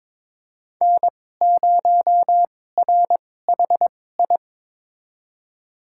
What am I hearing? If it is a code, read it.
Morse code, heard as a keyed tone, about 22 wpm, spaced as for 16 wpm, 700 Hz. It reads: N0RHI
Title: USA Call Signs - 5 Characters - 2x Character Spacing - 22wpm